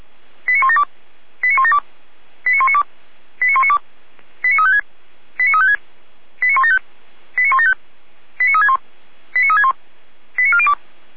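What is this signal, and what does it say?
Five-tone selective call (ZVEI), heard as quick runs of five tones, tones of about 70 ms. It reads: 89172 89172 89192 89192 89356 89356 89175 89175 89361 89361 89402
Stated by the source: ZVEI five-tone call received by SDR.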